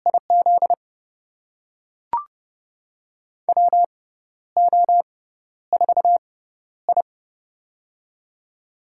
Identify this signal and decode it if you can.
Morse code, heard as a keyed tone, about 30 wpm, spaced as for 12 wpm, 700 Hz. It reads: IZ WO4I
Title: USA Call Signs - 4 Characters - 6x Character Spacing - 30wpm